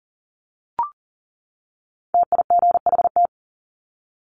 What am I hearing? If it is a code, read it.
Morse code, heard as a keyed tone, about 39 wpm, 700 Hz. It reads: TIGHT
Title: Top 1751-2000 Words 40wpm